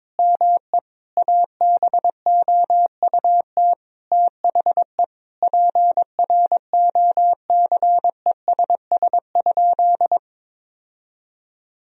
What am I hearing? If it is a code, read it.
Morse code, heard as a keyed tone, about 22 wpm, 700 Hz. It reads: ME ABOUT THE PROCESS?
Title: Sentences from Top 1000 Words - Review Part 3 - 22wpm